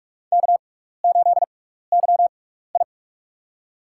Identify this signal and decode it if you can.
Morse code, heard as a keyed tone, about 44 wpm, spaced as for 15 wpm, 700 Hz. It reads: K8YI
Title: USA Call Signs - 6x Character Spacing - 45wpm